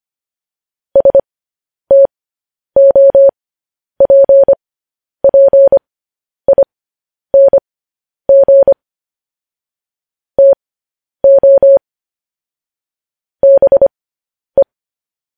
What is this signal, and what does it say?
Morse code, heard as a keyed tone, about 25 wpm, spaced as for 10 wpm, 550 Hz. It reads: STOPPING TO BE
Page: news headlines in morse code at 10 wpm